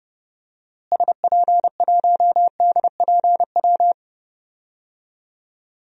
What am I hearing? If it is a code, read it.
Morse code, heard as a keyed tone, about 30 wpm, 700 Hz. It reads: SP1DPW
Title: DXCC-100 Call Signs - 6 Characters - Repeated 2x - 30wpm